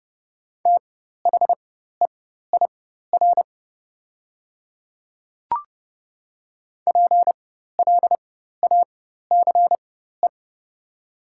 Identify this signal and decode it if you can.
Morse code, heard as a keyed tone, about 30 wpm, spaced as for 14 wpm, 690 Hz. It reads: THEIR PLACE
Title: 4x Character Spacing - Top 200 Words - 30wpm